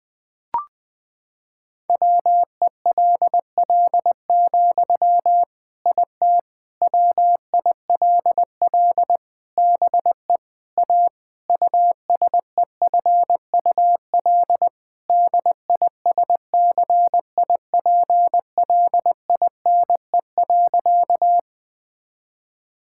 Morse code, heard as a keyed tone, about 21 wpm, 700 Hz. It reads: WELL, IT WILL BE A USEFUL DISCIPLINE.
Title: Sentences from Top 1000 Words - Review Part 6 - 20wpm